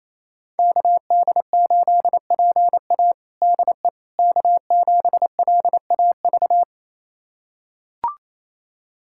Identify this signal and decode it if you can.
Morse code, heard as a keyed tone, about 28 wpm, 700 Hz. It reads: KD8PA DE K7LAV
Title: USA Callsign DE Callsign - 5 Characters - 28wpm